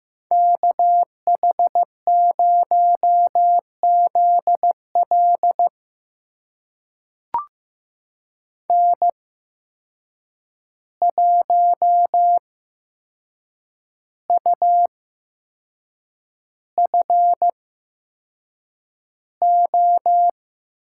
Morse code, heard as a keyed tone, about 15 wpm, spaced as for 6 wpm, 700 Hz. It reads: KH0ZL N1UFO
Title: USA Call Signs - 8x Character Spacing - 15wpm